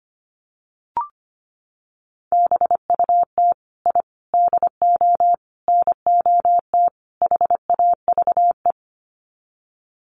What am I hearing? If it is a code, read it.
Morse code, heard as a keyed tone, about 25 wpm, 700 Hz. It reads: BUT I DO NOT HAVE
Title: Sets of 5 Words Encore 25wpm